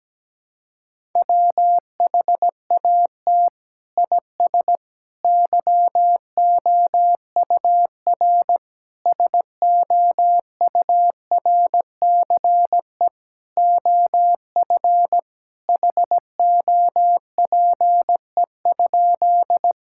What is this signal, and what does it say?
Morse code, heard as a keyed tone, about 17 wpm, 700 Hz. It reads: WHAT IS YOUR SOURCE OF HOPE?
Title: Sentences from Top 1000 Words - Review Part 5 - 17wpm